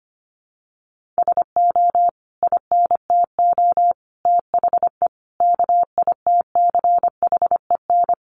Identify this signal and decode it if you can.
Morse code, heard as a keyed tone, about 25 wpm, 700 Hz. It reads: SO INTO THE KITCHEN
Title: Sentences from Top 1400 Words 25wpm